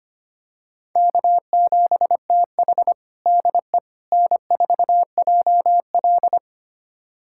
Morse code, heard as a keyed tone, about 25 wpm, 700 Hz. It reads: K7TH DE N4JL
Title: USA Callsign DE Callsign - 4 Characters - 25wpm